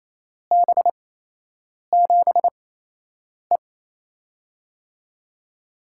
Morse code, heard as a keyed tone, about 28 wpm, spaced as for 8 wpm, 700 Hz. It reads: B7E
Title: USA Call Signs - 8x Character Spacing - 28wpm